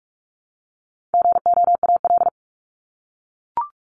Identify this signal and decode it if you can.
Morse code, heard as a keyed tone, about 44 wpm, 700 Hz. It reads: GOAL
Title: Top 751-1000 Words 45wpm